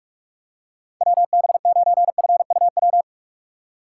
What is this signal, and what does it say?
Morse code, heard as a keyed tone, about 42 wpm, 700 Hz. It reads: WB9FUW